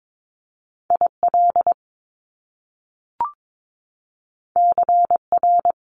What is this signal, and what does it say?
Morse code, heard as a keyed tone, about 22 wpm, 700 Hz. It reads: IL CR
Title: Top Two Letter Combinations in Words 22wpm